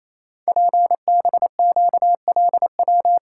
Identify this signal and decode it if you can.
Morse code, heard as a keyed tone, about 29 wpm, 700 Hz. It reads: PBQLW
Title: Code Groups - 5 Letters - 28wpm